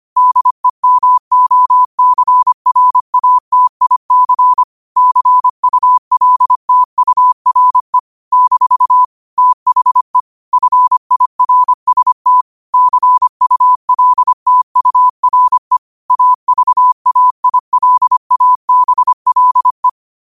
Morse code, heard as a keyed tone, about 25 wpm, 1000 Hz. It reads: DEMOCRATIC CULTURE - THE FIRST CULTURE AVAILABLE
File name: qotdfiles / 25wpm